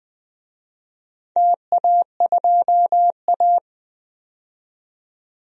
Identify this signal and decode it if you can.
Morse code, heard as a keyed tone, about 20 wpm, 700 Hz. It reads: TA2A
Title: DXCC-100 Call Signs - 4 Characters - 20wpm